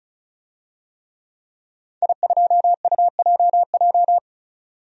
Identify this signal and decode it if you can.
Morse code, heard as a keyed tone, about 35 wpm, 700 Hz. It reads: I2UJJ
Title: DXCC-100 Call Signs - 5 Characters - 35wpm